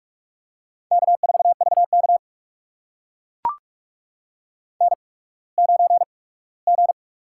Morse code, heard as a keyed tone, about 44 wpm, spaced as for 21 wpm, 700 Hz. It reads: K4VK N9G
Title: USA Call Signs - 4 Characters - 8x Character Spacing - 45wpm